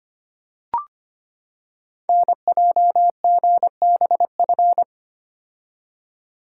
Morse code, heard as a keyed tone, about 25 wpm, 700 Hz. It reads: NJGBF
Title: Code Groups - 5 Letters - 25wpm